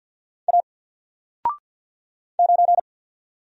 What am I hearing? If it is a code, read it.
Morse code, heard as a keyed tone, about 50 wpm, 700 Hz. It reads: A 9
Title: Single Letter-Number 50wpm